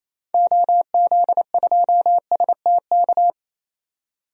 Morse code, heard as a keyed tone, about 28 wpm, 700 Hz. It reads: OZ2STK
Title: DXCC-100 Call Signs - 6 Characters - 28wpm